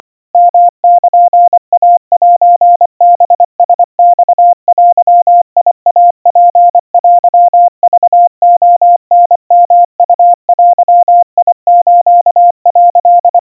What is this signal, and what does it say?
Morse code (Japanese Wabun code), heard as a keyed tone, about 24 wpm, 700 Hz. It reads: ヨルイセハラマデイツテクレタヨウデス」
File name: 90 cpm 700 Hz Crispy